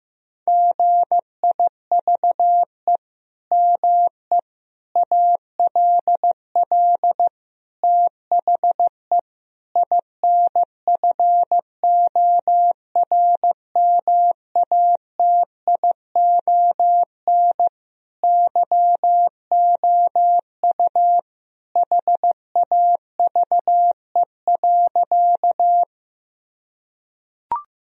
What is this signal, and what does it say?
Morse code, heard as a keyed tone, about 16 wpm, 700 Hz. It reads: GIVE ME ALL THE INFORMATION YOU HAVE.
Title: Sentences from Top 1000 Words - Review Part 3 - 15wpm